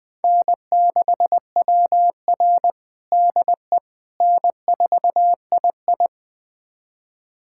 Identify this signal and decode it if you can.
Morse code, heard as a keyed tone, about 21 wpm, 700 Hz. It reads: N6WR DE N4II
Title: USA Callsign DE Callsign - 4 Characters - 20wpm